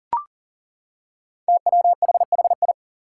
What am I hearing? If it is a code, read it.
Morse code, heard as a keyed tone, about 40 wpm, 700 Hz. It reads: TWHHI
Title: Five Challenging Letter-Number Combinations 40wpm